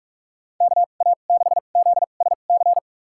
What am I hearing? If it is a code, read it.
Morse code, heard as a keyed tone, about 46 wpm, spaced as for 33 wpm, 700 Hz. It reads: KA6ZSC
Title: USA Call Signs - 6 Characters - 2x Character Spacing - 45wpm